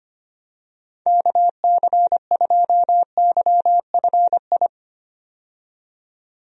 Morse code, heard as a keyed tone, about 25 wpm, 700 Hz. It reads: KC2YFI
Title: USA Call Signs - 6 Characters - 25wpm